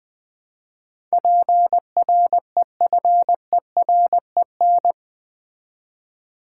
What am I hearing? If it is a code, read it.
Morse code, heard as a keyed tone, about 20 wpm, 700 Hz. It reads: PREFEREN
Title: Build A Word - 4 Letters At A Time - 10-Letter Words - 20wpm